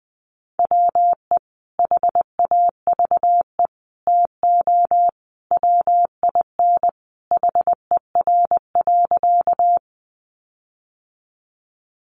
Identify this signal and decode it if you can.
Morse code, heard as a keyed tone, about 20 wpm, 700 Hz. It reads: WE HAVE TO WIN HER.
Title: Sentences from Top 1500 Words 20wpm